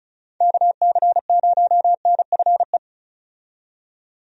Morse code, heard as a keyed tone, about 34 wpm, 700 Hz. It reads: KC0NFE
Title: USA Call Signs - 6 Characters - 35wpm